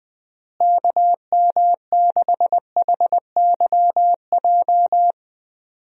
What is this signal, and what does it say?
Morse code, heard as a keyed tone, about 21 wpm, 700 Hz. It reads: KM6HYJ
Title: USA Call Signs - 6 Characters - Repeated 3x - 20wpm